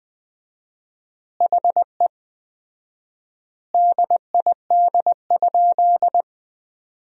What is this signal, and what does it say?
Morse code, heard as a keyed tone, about 20 wpm, 700 Hz. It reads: HE DID?
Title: 4x Word-Spacing - Sentences from Top 100 Words - 20wpm